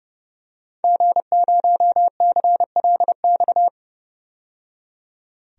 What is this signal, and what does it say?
Morse code, heard as a keyed tone, about 30 wpm, 700 Hz. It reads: G0CLX
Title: DXCC-100 Call Signs - 5 Characters - 30wpm